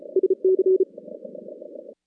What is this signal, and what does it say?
Morse code, heard as a keyed tone, about 34 wpm, 380 Hz. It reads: SC
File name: contests / CW-SS-k5tr- -2100